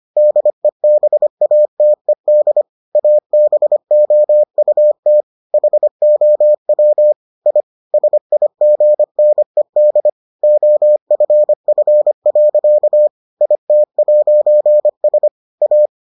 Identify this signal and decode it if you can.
Morse code, heard as a keyed tone, about 25 wpm, 600 Hz. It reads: DEBATED ABOUT HOW I SIGNED OFF. IT'S A